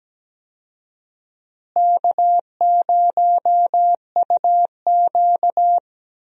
Morse code, heard as a keyed tone, about 17 wpm, 700 Hz. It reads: K0UQ
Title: USA Call Signs - 4 Characters - Repeated 3x - 17wpm